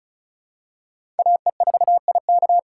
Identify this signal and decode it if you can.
Morse code, heard as a keyed tone, about 36 wpm, 700 Hz. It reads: AE4IK
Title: USA Call Signs - 5 Characters - 4x Character Spacing - 35wpm